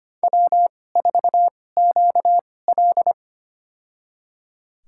Morse code, heard as a keyed tone, about 25 wpm, spaced as for 18 wpm, 700 Hz. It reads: W4QL
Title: USA Call Signs - 4 Characters - 2x Character Spacing - 25wpm